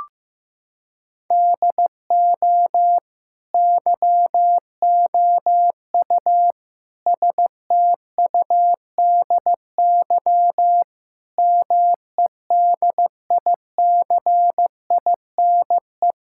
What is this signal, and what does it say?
Morse code, heard as a keyed tone, about 15 wpm, 700 Hz. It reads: DO YOU STUDY MEDICINE
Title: Sentences from Top 1000 Words - Review Part 6 - 15wpm